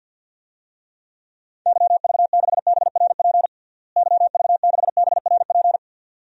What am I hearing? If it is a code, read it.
Morse code, heard as a keyed tone, about 51 wpm, 700 Hz. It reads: YV6BRP YV6BRP
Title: DXCC-100 Call Signs - 6 Characters - Repeated 2x - 50wpm